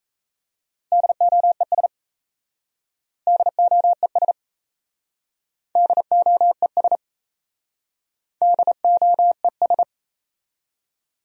Morse code, heard as a keyed tone, about 37 wpm, 700 Hz. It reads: DOES DOES DOES DOES